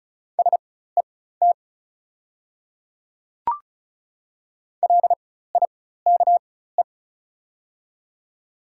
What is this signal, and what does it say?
Morse code, heard as a keyed tone, about 36 wpm, spaced as for 16 wpm, 700 Hz. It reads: SET LIKE
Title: 4x Character Spacing - Top 500 Words - 35wpm